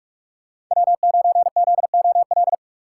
Morse code, heard as a keyed tone, about 46 wpm, 700 Hz. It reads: W9ZOL
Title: USA Call Signs - 5 Characters - 6x Character Spacing - 45wpm